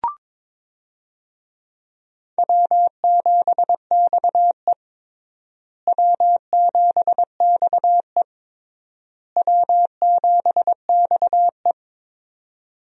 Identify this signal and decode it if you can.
Morse code, heard as a keyed tone, about 22 wpm, 700 Hz. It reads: W7XE W7XE W7XE
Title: USA Call Signs - 4 Characters - Repeated 3x - 22wpm